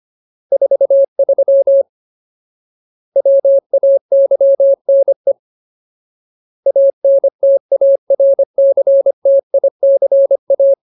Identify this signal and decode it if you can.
Morse code, heard as a keyed tone, about 25 wpm, 550 Hz. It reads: 43 WAYNE ANTARCTICA
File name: Ultimate Word List / 25WPM Ultimate